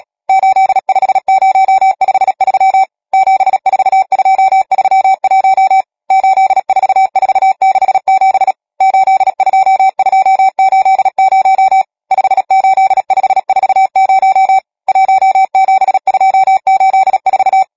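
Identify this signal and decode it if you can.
Morse code, heard as a keyed tone, about 37 wpm, 770 Hz. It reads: 85053 74231 84467 82280 58540 17284
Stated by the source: Morse code number practice, machine-sent training text (Nácvik telegrafie, OK1FCB).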